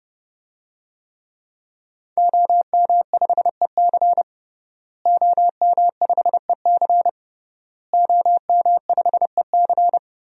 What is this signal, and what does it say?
Morse code, heard as a keyed tone, about 30 wpm, 700 Hz. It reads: OM5EC OM5EC OM5EC
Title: DXCC-100 Call Signs - 5 Characters - Repeated 3x - 30wpm